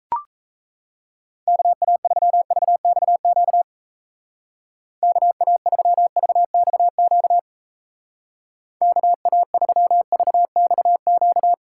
Morse code, heard as a keyed tone, about 34 wpm, 700 Hz. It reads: KA3VXQ KA3VXQ KA3VXQ
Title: Speed Racing - USA Call Signs 28wpm